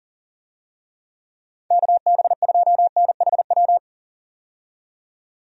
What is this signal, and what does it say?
Morse code, heard as a keyed tone, about 40 wpm, 700 Hz. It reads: KB2NHW
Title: USA Call Signs - 6 Characters - 40wpm